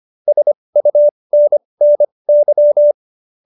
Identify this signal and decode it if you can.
Morse code, heard as a keyed tone, about 25 wpm, spaced as for 20 wpm, 590 Hz. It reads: SUNNY